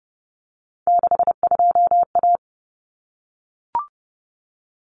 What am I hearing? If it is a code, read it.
Morse code, heard as a keyed tone, about 30 wpm, 700 Hz. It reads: 62A
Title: Common Cut Numbers - 3-Digit 30wpm